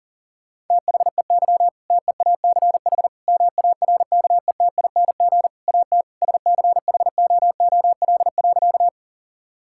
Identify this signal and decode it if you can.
Morse code, heard as a keyed tone, about 40 wpm, 700 Hz. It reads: THEY TEACH MARKETING AT SCHOOL.
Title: Sentences from Top 1000 Words - Review Part 5 - 40wpm